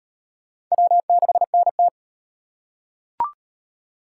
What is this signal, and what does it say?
Morse code, heard as a keyed tone, about 38 wpm, 700 Hz. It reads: W6NT